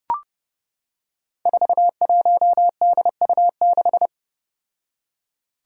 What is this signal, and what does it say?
Morse code, heard as a keyed tone, about 29 wpm, 700 Hz. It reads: 41DU6
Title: Five Challenging Letter-Number Combinations 30wpm